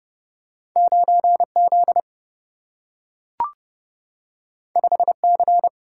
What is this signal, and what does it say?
Morse code, heard as a keyed tone, about 30 wpm, 700 Hz. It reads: 9Z 5C
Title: Single-Letter Single-Number 30wpm